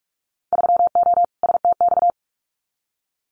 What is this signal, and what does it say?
Morse code, heard as a keyed tone, about 46 wpm, 700 Hz. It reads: 3O STX
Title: Winter Field Day Exchanges 45wpm